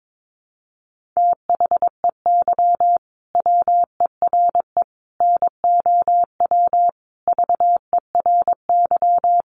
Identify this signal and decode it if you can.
Morse code, heard as a keyed tone, about 22 wpm, 700 Hz. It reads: THEY WERE NOW VERY
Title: Sentences from Top 1400 Words 22wpm